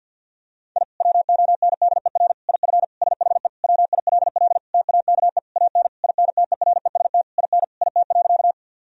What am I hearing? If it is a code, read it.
Morse code, heard as a keyed tone, about 50 wpm, 700 Hz. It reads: I WONDER IF SHE WILL TAKE AN INTEREST IN IT.
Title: Sentences from Top 1000 Words - Review Part 3 - 50wpm